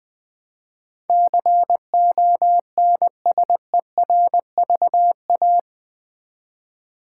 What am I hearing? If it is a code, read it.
Morse code, heard as a keyed tone, about 19 wpm, 700 Hz. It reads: CONSERVA
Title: Build A Word - 4 Letters At A Time - 12-Letter Words - 20wpm